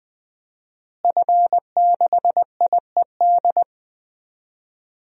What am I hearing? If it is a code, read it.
Morse code, heard as a keyed tone, about 20 wpm, 700 Hz. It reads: F6IED
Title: DXCC-100 Call Signs - 5 Characters - Repeated 2x - 20wpm